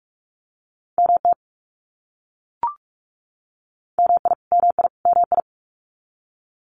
Morse code, heard as a keyed tone, about 44 wpm, 700 Hz. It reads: MT MI MI MI
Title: US States Repeated 3x 45wpm